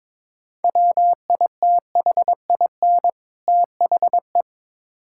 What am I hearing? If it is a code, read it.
Morse code, heard as a keyed tone, about 22 wpm, 700 Hz. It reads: WITHIN THE
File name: Build A Phrase - 2-Word Phrases - 22wpm